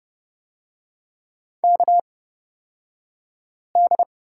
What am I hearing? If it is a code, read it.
Morse code, heard as a keyed tone, about 31 wpm, spaced as for 5 wpm, 690 Hz. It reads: KD
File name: Lesson 36d - BT - Two Characters 30wpm